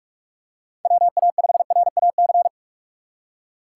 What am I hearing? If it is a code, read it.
Morse code, heard as a keyed tone, about 41 wpm, 700 Hz. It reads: WA5RAC
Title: USA Call Signs - 6 Characters - 3x Character Spacing - 45wpm